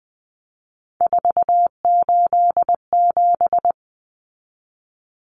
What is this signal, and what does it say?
Morse code, heard as a keyed tone, about 20 wpm, 700 Hz. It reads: 487
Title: Numbers - 3-Digit 20wpm